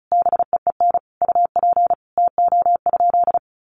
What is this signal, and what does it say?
Morse code, heard as a keyed tone, about 34 wpm, 700 Hz. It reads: BEEN UP TO?
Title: Sentences from Top 100 Words 35wpm